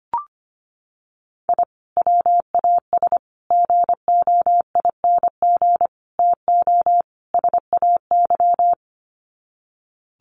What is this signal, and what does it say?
Morse code, heard as a keyed tone, about 25 wpm, 700 Hz. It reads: I WAS GOING TO SAY
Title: Sets of 5 Words 25wpm